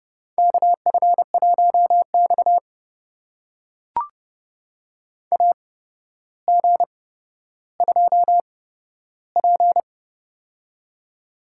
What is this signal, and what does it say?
Morse code, heard as a keyed tone, about 30 wpm, spaced as for 13 wpm, 700 Hz. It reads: KF1X AG2P
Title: USA Call Signs - 8x Character Spacing - 30wpm